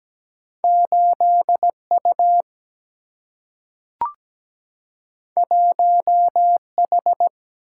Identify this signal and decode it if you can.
Morse code, heard as a keyed tone, about 17 wpm, 700 Hz. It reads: 8U 1H
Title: Single-Letter Single-Number 17wpm